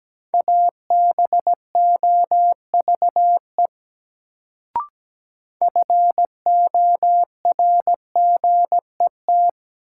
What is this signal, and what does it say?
Morse code, heard as a keyed tone, about 17 wpm, 700 Hz. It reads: ABOVE FORGET
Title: Sentences from Top 1200 Words - New Word Drill Alt - 17wpm